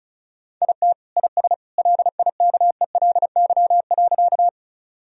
Morse code, heard as a keyed tone, about 34 wpm, 700 Hz. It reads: IT IS LIKELY.
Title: Sentences from Top 1000 Words - Review Part 4 - 35wpm